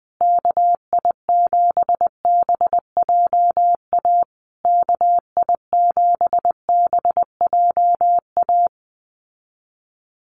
Morse code, heard as a keyed tone, about 19 wpm, 700 Hz. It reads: KI7BJA KI7BJA
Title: USA Call Signs Repeated 3x 20wpm